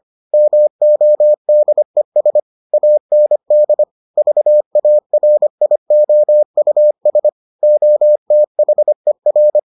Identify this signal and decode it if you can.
Morse code, heard as a keyed tone, about 25 wpm, 600 Hz. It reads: MODES AND VARIOUS OTHER